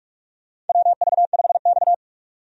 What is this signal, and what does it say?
Morse code, heard as a keyed tone, about 46 wpm, 700 Hz. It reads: WV5X
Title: USA Call Signs - 4 Characters - 45wpm